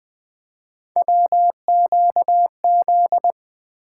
Morse code, heard as a keyed tone, about 20 wpm, 700 Hz. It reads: WQZ
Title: Three Letter-Number Combinations - 20wpm